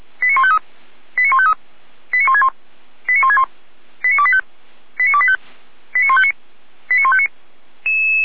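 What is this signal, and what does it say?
Five-tone selective call (ZVEI), heard as quick runs of five tones, tones of about 70 ms. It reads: 89153 89153 89161 89161 89285 89285 89159 89159
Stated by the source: ZVEI five-tone selective call received on an SDR at Feuerwehr Orlamünde.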